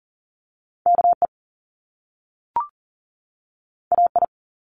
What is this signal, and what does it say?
Morse code, heard as a keyed tone, about 40 wpm, 700 Hz. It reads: KE AI
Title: Top Two Letter Combinations in Words 40wpm